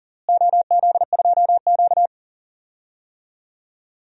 Morse code, heard as a keyed tone, about 38 wpm, 700 Hz. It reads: OZ2Q
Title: DXCC-100 Call Signs - 4 Characters - Repeated 3x - 40wpm